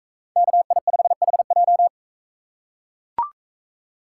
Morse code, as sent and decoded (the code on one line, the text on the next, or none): -.- .. ..... .... .---
KI5HJ